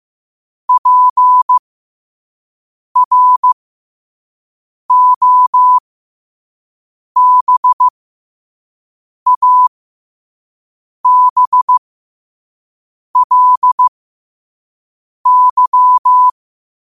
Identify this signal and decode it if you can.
Morse code, heard as a keyed tone, about 15 wpm, spaced as for 5 wpm, 1000 Hz. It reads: PROBABLY
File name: qotdfiles / 5wpm